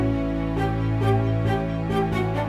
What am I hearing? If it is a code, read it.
There is no code.